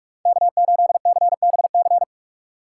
Morse code, heard as a keyed tone, about 59 wpm, 700 Hz. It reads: K8CBC